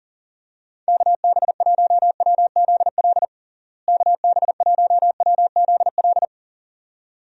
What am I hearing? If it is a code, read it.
Morse code, heard as a keyed tone, about 40 wpm, 700 Hz. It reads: KB1WZL KB1WZL
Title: USA Call Signs - 6 Characters - Repeated 2x - 40wpm